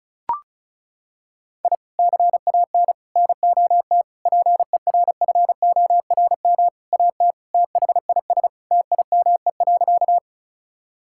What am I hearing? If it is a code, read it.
Morse code, heard as a keyed tone, about 36 wpm, 700 Hz. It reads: I CAN NOT PERFORM AT THIS TIME.
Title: Sentences from Top 1000 Words - Review Part 5 - 35wpm